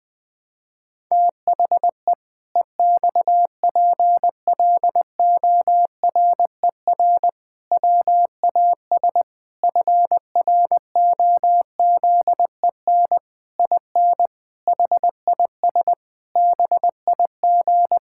Morse code, heard as a keyed tone, about 20 wpm, 700 Hz. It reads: THE EXPLORER WAS FROZEN IN HIS BIG
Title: Pangram Challenge v2 - 20wpm